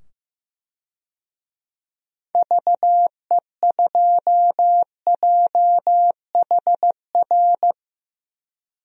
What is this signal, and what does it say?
Morse code, heard as a keyed tone, about 15 wpm, 700 Hz. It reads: VE2JHR
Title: DXCC-100 Call Signs - 6 Characters - 15wpm